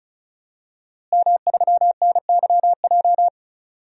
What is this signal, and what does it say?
Morse code, heard as a keyed tone, about 36 wpm, 700 Hz. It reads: M3NYJ
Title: DXCC-100 Call Signs - 5 Characters - Repeated 2x - 35wpm